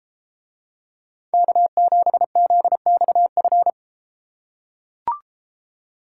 Morse code, heard as a keyed tone, about 30 wpm, 700 Hz. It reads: K7ZXF